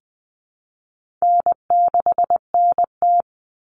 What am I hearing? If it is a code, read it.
Morse code, heard as a keyed tone, about 20 wpm, 700 Hz. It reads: N6NT
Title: Common Cut Numbers - 4-Digit 20wpm